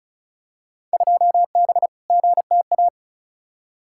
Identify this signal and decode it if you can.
Morse code, heard as a keyed tone, about 34 wpm, 700 Hz. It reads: 2B GTA
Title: Field Day Exchanges Encore - 35wpm